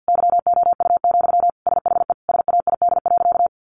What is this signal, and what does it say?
Morse code, heard as a keyed tone, about 50 wpm, 700 Hz. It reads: YOU, SHE SAID.
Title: Sentences from Top 400 Words 50wpm